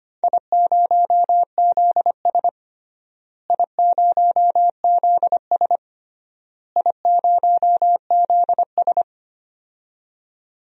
Morse code, heard as a keyed tone, about 25 wpm, 700 Hz. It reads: I0ZS I0ZS I0ZS
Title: DXCC-100 Call Signs - 4 Characters - Repeated 3x - 25wpm